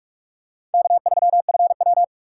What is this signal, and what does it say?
Morse code, heard as a keyed tone, about 43 wpm, 690 Hz. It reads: K3FW
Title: USA Call Signs - 4 Characters - 2x Character Spacing - 45wpm